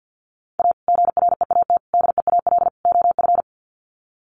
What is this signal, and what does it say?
Morse code, heard as a keyed tone, about 50 wpm, 700 Hz. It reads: A GREAT DEAL OF
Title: Sets of 4 Words 50wpm